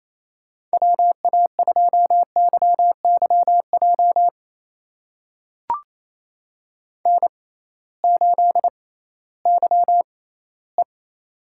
Morse code, heard as a keyed tone, about 28 wpm, spaced as for 16 wpm, 700 Hz. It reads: WA2YYJ N8YE
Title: USA Call Signs - 6x Character Spacing - 28wpm